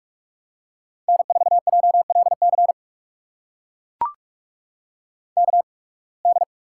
Morse code, heard as a keyed tone, about 44 wpm, spaced as for 29 wpm, 700 Hz. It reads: N4JLC KD
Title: USA Call Signs - 5 Characters - 8x Character Spacing - 45wpm